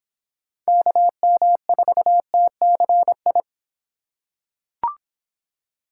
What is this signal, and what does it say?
Morse code, heard as a keyed tone, about 25 wpm, 700 Hz. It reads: KM4TCI